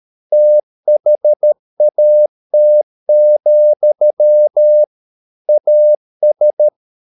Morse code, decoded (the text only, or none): THAT, AS